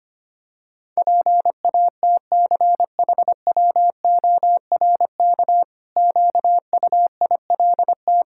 Morse code, heard as a keyed tone, about 25 wpm, 700 Hz. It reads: PATCHWORK QUILT
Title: Pangram Challenge v2 - 25wpm